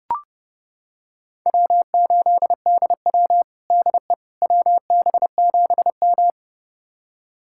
Morse code, heard as a keyed tone, about 30 wpm, 700 Hz. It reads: W8DW DE WB7M
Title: USA Callsign DE Callsign - 4 Characters - 30wpm